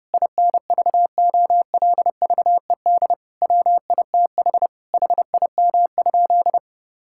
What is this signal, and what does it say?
Morse code, heard as a keyed tone, about 32 wpm, 700 Hz. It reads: INVOLVED WITH HIM?
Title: Sentences from Top 1000 Words - Review Part 5 - 30wpm